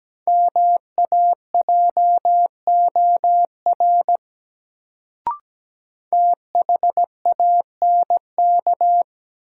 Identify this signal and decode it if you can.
Morse code, heard as a keyed tone, about 17 wpm, 700 Hz. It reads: MAJOR THANK
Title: Sentences from Top 600 Words - New Word Drill - 17wpm